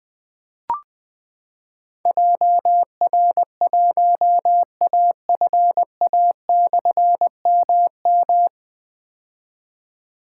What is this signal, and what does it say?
Morse code, heard as a keyed tone, about 20 wpm, 700 Hz. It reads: JR1AFA/MM
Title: Difficult Call Signs 20wpm